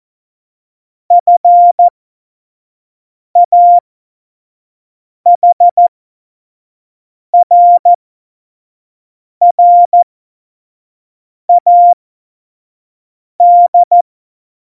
Morse code, heard as a keyed tone, about 14 wpm, spaced as for 5 wpm, 700 Hz. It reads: FAHRRAD
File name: Morsekurs / Witze / Witz 26. Februar 2019, 25 F70